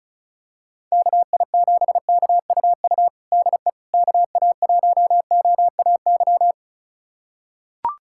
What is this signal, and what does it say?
Morse code, heard as a keyed tone, about 36 wpm, 700 Hz. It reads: KI7KUU DE KA1OAY
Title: USA Callsign DE Callsign 35wpm